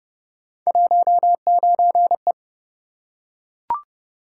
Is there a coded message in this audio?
Yes: Morse code from 0.7 to 2.3 s, heard as a keyed tone, about 29 wpm, 700 Hz. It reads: 19E